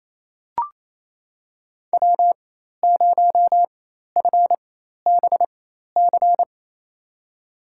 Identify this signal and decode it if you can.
Morse code, heard as a keyed tone, about 28 wpm, spaced as for 13 wpm, 700 Hz. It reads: W0FBC
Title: USA Call Signs - 5 Characters - 4x Character Spacing - 28wpm